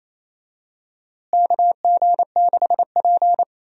Morse code, heard as a keyed tone, about 28 wpm, 700 Hz. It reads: KG6P
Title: USA Call Signs - 4 Characters - Repeated 2x - 28wpm